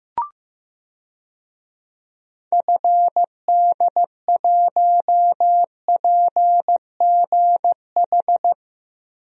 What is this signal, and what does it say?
Morse code, heard as a keyed tone, about 15 wpm, 700 Hz. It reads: FD1PGH
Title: DXCC-100 Call Signs - 6 Characters - 15wpm